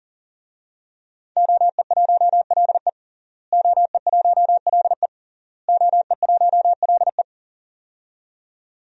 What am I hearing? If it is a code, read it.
Morse code, heard as a keyed tone, about 39 wpm, 700 Hz. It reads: OE1LE OE1LE OE1LE
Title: DXCC-100 Call Signs - 5 Characters - Repeated 3x - 40wpm